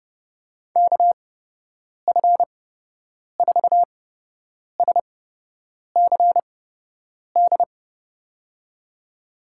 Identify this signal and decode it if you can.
Morse code, heard as a keyed tone, about 31 wpm, spaced as for 8 wpm, 700 Hz. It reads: KF4SCD
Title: USA Call Signs - 6 Characters - 8x Character Spacing - 30wpm